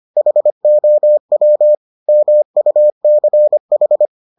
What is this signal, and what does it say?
Morse code, heard as a keyed tone, about 25 wpm, 600 Hz. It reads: HOW MUCH